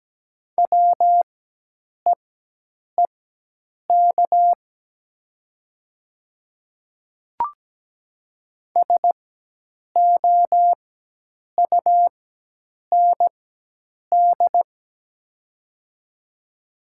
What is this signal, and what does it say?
Morse code, heard as a keyed tone, about 17 wpm, spaced as for 8 wpm, 700 Hz. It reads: WEEK SOUND
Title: 4x Character Spacing - Top 500 Words - 17wpm